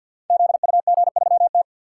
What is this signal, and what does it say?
Morse code, heard as a keyed tone, about 49 wpm, 700 Hz. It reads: ZUG3T
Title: Code Groups - 5 Letter:Numbers - 50wpm